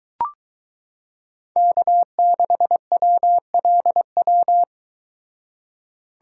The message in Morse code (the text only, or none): K6WLW